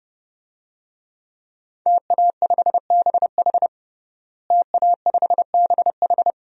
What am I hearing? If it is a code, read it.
Morse code, heard as a keyed tone, about 30 wpm, 700 Hz. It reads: TA5BH TA5BH
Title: DXCC-100 Call Signs - 5 Characters - Repeated 2x - 30wpm